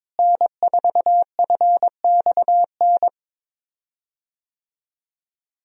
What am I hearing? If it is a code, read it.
Morse code, heard as a keyed tone, about 22 wpm, 700 Hz. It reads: N4FXN
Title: USA Call Signs - 5 Characters - Repeated 2x - 22wpm